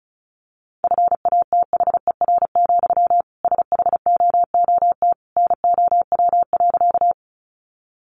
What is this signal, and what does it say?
Morse code, heard as a keyed tone, about 34 wpm, 700 Hz. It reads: FATHER, SHOOT NOW.
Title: Sentences from Top 1200 Words 35wpm